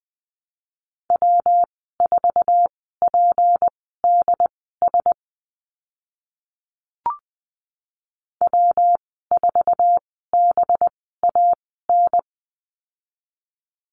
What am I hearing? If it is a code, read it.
Morse code, heard as a keyed tone, about 20 wpm, spaced as for 14 wpm, 700 Hz. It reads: W4PDS W4BAN
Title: 193 Lesson 31g - Letter v - Call Signs 20wpm